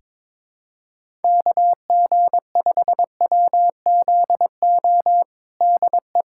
Morse code, heard as a keyed tone, about 22 wpm, 700 Hz. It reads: KG5WZO DE
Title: USA Callsign DE Callsign 22wpm